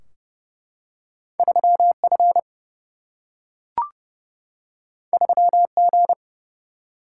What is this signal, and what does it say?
Morse code, heard as a keyed tone, about 30 wpm, 700 Hz. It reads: 3F 3G
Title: Single-Letter Single-Number 30wpm